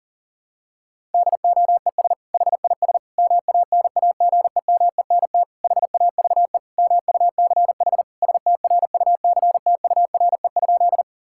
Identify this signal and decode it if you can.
Morse code, heard as a keyed tone, about 40 wpm, 700 Hz. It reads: DOES HIS MANAGEMENT HAVE MUCH STRUCTURE?
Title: Sentences from Top 1000 Words - Review Part 6 - 40wpm